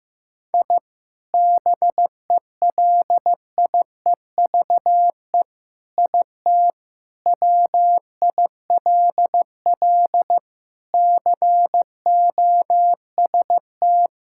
Morse code, heard as a keyed tone, about 15 wpm, 700 Hz. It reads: I BELIEVE IT WILL COST
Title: Sentences from Top 1000 Words - Review Part 5 - 15wpm